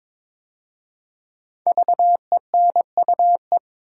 Morse code, heard as a keyed tone, about 22 wpm, 700 Hz. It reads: VENUE
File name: 5 Letter Words - First 3 Letters Given - 22wpm